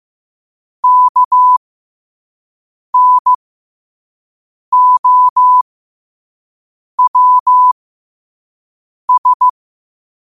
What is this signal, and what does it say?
Morse code, heard as a keyed tone, about 15 wpm, spaced as for 5 wpm, 1000 Hz. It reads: KNOWS
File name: 5wpm